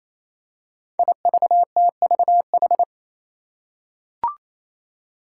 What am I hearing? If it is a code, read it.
Morse code, heard as a keyed tone, about 28 wpm, 700 Hz. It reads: IVTVH
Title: Five Challenging Letter-Number Combinations 28wpm